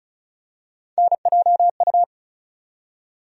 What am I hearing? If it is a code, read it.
Morse code, heard as a keyed tone, about 36 wpm, 700 Hz. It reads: NJU
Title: USA Call Sign Postfixes - 3 Letters 35wpm